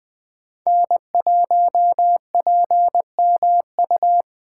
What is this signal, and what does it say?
Morse code, heard as a keyed tone, about 20 wpm, 700 Hz. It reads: N1PMU
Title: USA Call Signs - 5 Characters - 4x Character Spacing - 20wpm